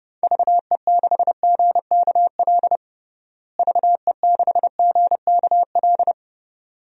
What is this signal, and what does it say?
Morse code, heard as a keyed tone, about 30 wpm, 700 Hz. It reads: VE6GKL VE6GKL
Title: DXCC-100 Call Signs - 6 Characters - Repeated 2x - 30wpm